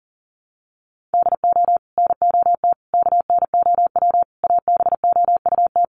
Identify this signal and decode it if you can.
Morse code, heard as a keyed tone, about 37 wpm, 700 Hz. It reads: DO NOT KNOW ABOUT
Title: Sets of 5 Words Encore2 40wpm